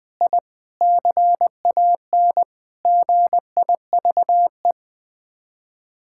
Morse code, heard as a keyed tone, about 20 wpm, 700 Hz. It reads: I CAN GIVE
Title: Sets of 8 Words - 20wpm